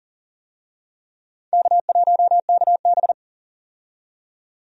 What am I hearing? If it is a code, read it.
Morse code, heard as a keyed tone, about 40 wpm, 700 Hz. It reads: K1KB
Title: USA Call Signs - 4 Characters - 40wpm